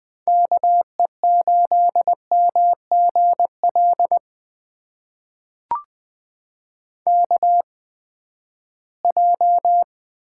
Morse code, heard as a keyed tone, about 20 wpm, spaced as for 14 wpm, 700 Hz. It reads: KE8MGL KJ
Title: USA Call Signs - 6 Characters - 8x Character Spacing - 20wpm